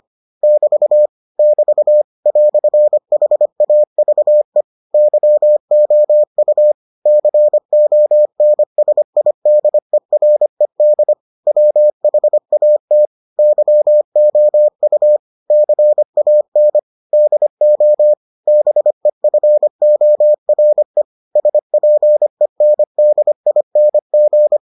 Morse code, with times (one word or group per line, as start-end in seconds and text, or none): 0.4-1.1 <BT>
1.4-2.0 <BT>
2.3-4.6 "HAVE
4.9-6.7 YOU
7.1-11.1 CONSIDERED
11.5-13.1 WHAT
13.4-15.2 YOU
15.5-16.8 CAN
17.1-18.1 DO
18.5-21.0 BEFORE
21.4-24.6 SPENDING